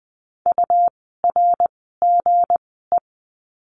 Morse code, heard as a keyed tone, about 20 wpm, spaced as for 15 wpm, 700 Hz. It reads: URGE
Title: 160 Lesson 26f - Letter g - Words 20wpm